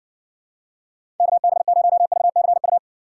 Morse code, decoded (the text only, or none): KB0VCU